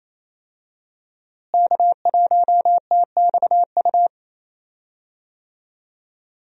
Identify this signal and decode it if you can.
Morse code, heard as a keyed tone, about 28 wpm, 700 Hz. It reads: K1TXU
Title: USA Call Signs - 5 Characters - Repeated 2x - 28wpm